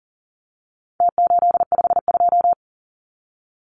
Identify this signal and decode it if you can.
Morse code, heard as a keyed tone, about 40 wpm, 700 Hz. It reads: T852
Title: Numbers - Contest Style - 4-Digit 40wpm